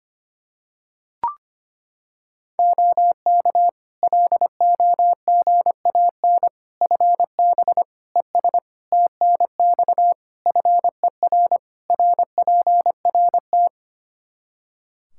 Morse code, heard as a keyed tone, about 25 wpm, 700 Hz. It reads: OK LOGAN FB ES TNX FER RPRT
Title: Practice QSO - Exchange 2 - Info - 25wpm